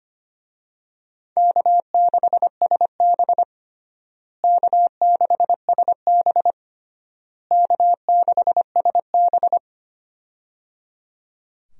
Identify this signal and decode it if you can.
Morse code, heard as a keyed tone, about 25 wpm, 700 Hz. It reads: K6SB K6SB K6SB
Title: USA Call Signs - 4 Characters - Repeated 3x - 25wpm